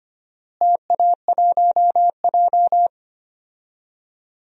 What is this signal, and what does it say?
Morse code, heard as a keyed tone, about 25 wpm, 700 Hz. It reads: TA1J
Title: DXCC-100 Call Signs - 4 Characters - Repeated 3x - 25wpm